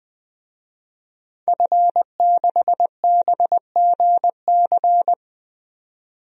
Morse code, heard as a keyed tone, about 21 wpm, 700 Hz. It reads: F6BGC
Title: DXCC-100 Call Signs - 5 Characters - 20wpm